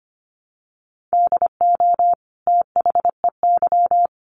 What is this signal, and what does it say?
Morse code, heard as a keyed tone, about 25 wpm, 700 Hz. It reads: DO THEY
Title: Sentences from Top 1300 Words 25wpm